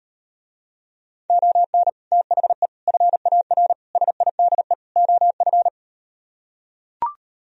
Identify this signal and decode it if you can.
Morse code, heard as a keyed tone, about 38 wpm, 700 Hz. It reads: ON THE FAR SIDE OF